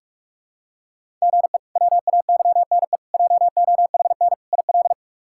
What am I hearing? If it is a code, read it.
Morse code, heard as a keyed tone, about 46 wpm, 700 Hz. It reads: GE WAYNE JOHN IL
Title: K1USN SST QSO Practice - 45wpm